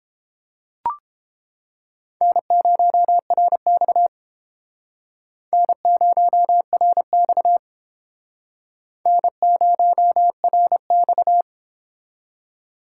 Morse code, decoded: N0RX N0RX N0RX

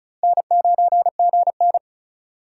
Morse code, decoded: N9GN

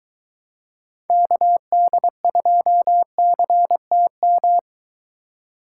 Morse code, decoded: KD2CTM

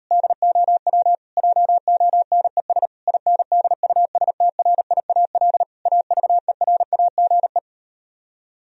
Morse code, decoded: DOW JONES INDUSTRIAL AVERAGE

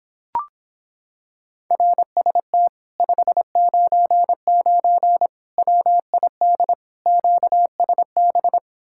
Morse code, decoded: RST 599 WID QSB

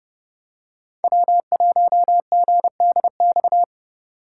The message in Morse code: W1GDX

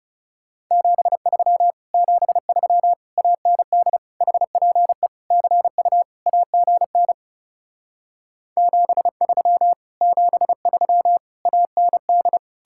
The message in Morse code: 73 73 AND HPE CU AGN 73 73 AND

